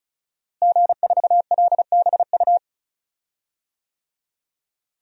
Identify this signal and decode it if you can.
Morse code, heard as a keyed tone, about 36 wpm, 700 Hz. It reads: G4LBU
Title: DXCC-100 Call Signs - 5 Characters - Repeated 3x - 35wpm